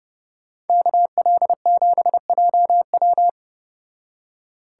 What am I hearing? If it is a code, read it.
Morse code, heard as a keyed tone, about 29 wpm, 700 Hz. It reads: KL7JW